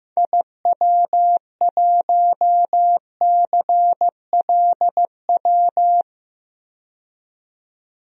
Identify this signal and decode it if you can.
Morse code, heard as a keyed tone, about 15 wpm, 700 Hz. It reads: IW1CLW
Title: DXCC-100 Call Signs - 6 Characters - 15wpm